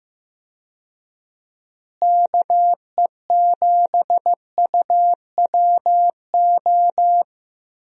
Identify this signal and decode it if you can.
Morse code, heard as a keyed tone, about 15 wpm, 700 Hz. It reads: KE7UWO
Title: USA Call Signs - 6 Characters - Repeated 3x - 15wpm